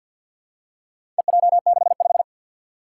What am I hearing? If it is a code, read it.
Morse code, heard as a keyed tone, about 51 wpm, 700 Hz. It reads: EJ65
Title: Four Challenging Letter-Number Combinations 50wpm